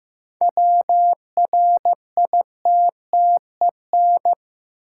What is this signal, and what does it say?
Morse code, heard as a keyed tone, about 15 wpm, 700 Hz. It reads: WRITTEN
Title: Sentences from Top 1500 Words - New Word Drill Alt - 15wpm